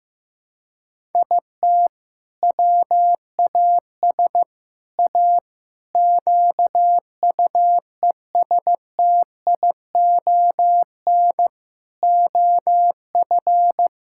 Morse code, heard as a keyed tone, about 15 wpm, 700 Hz. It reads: IT WAS A QUESTION OF
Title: Sentences from Top 1000 Words - Review Part 6 - 15wpm